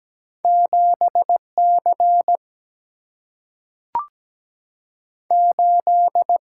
Morse code, heard as a keyed tone, about 17 wpm, spaced as for 3 wpm, 700 Hz. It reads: 7 C8
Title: Single-Letter Single-Number 17wpm